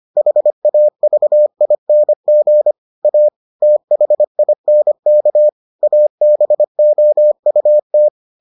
Morse code, heard as a keyed tone, about 25 wpm, 600 Hz. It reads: HAVING A THINK ABOUT